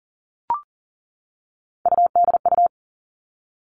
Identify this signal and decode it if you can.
Morse code, heard as a keyed tone, about 40 wpm, 700 Hz. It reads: UDU
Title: Three Challenging Letter-Number Combinations 40wpm